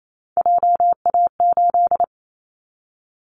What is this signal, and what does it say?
Morse code, heard as a keyed tone, about 28 wpm, 700 Hz. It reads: JA8
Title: Callsign Prefixes for Top 25 DXCC - 28wpm